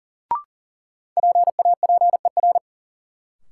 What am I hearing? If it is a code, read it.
Morse code, heard as a keyed tone, about 40 wpm, 700 Hz. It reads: PAPER
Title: Sentences from Top 300 Words - New Word Drill - 40wpm